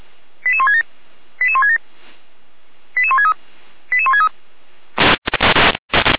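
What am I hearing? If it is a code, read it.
Five-tone selective call (ZVEI), heard as quick runs of five tones, tones of about 70 ms. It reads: 88167 88167 88163 88163
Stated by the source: Received by SDR and recorded by the Feuerwehr Orlamünde fire brigade.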